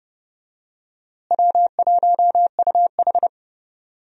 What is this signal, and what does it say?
Morse code, heard as a keyed tone, about 32 wpm, 700 Hz. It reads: W1UH
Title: USA Call Signs - 4 Characters - Repeated 2x - 30wpm